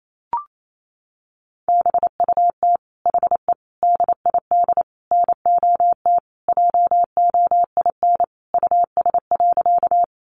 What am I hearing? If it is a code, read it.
Morse code, heard as a keyed tone, about 28 wpm, 700 Hz. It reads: BUT HE DID NOT JOIN US.
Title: Sentences from Top 1400 Words 28wpm